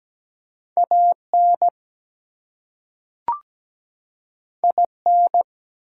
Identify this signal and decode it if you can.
Morse code, heard as a keyed tone, about 17 wpm, 700 Hz. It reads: AN IN
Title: Two Letter Words - 17wpm